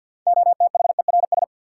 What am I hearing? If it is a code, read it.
Morse code, heard as a keyed tone, about 52 wpm, 700 Hz. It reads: OTHERS
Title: Sentences from Top 1000 Words - Review Part 2 - New Word Drill - 50wpm